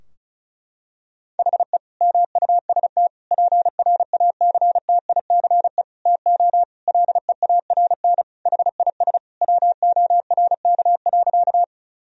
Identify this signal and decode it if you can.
Morse code, heard as a keyed tone, about 35 wpm, 700 Hz. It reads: HE MUST PRACTICE TO LEARN HIS WORK.
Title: Sentences from Top 1000 Words - Review Part 4 - 35wpm